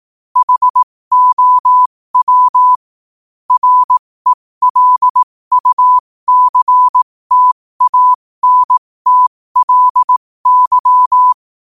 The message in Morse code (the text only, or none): HOW RELUCTANTLY